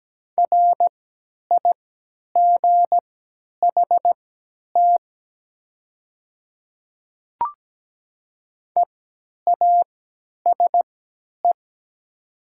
Morse code, heard as a keyed tone, about 17 wpm, spaced as for 10 wpm, 700 Hz. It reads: RIGHT EASE
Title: 3x Character Spacing - Top 300 Words - 17wpm